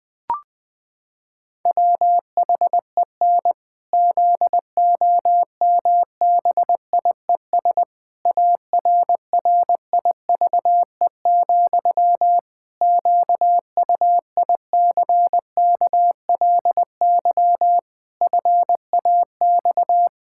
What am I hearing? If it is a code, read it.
Morse code, heard as a keyed tone, about 19 wpm, 700 Hz. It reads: WHEN ZOMBIES ARRIVE, QUICKLY FAX
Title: Pangram Challenge v2 - 20wpm